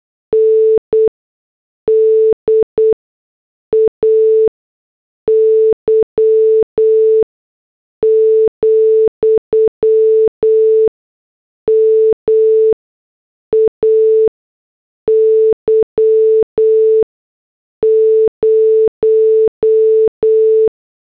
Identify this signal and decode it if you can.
Morse code, heard as a keyed tone, about 8 wpm, 440 Hz. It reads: N D A Y , M A Y 0
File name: morse / headlines